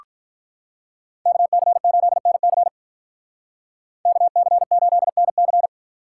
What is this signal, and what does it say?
Morse code, heard as a keyed tone, about 46 wpm, 700 Hz. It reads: KC8NC KC8NC